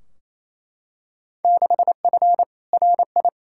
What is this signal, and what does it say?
Morse code, heard as a keyed tone, about 28 wpm, 700 Hz. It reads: 6F RI
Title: Field Day Exchanges Encore - 28wpm